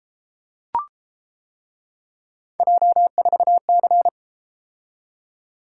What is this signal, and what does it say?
Morse code, heard as a keyed tone, about 30 wpm, 700 Hz. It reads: J4C